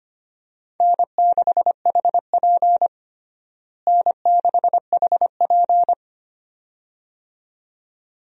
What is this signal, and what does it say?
Morse code, heard as a keyed tone, about 25 wpm, 700 Hz. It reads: N6HP N6HP
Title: USA Call Signs - 4 Characters - Repeated 2x - 25wpm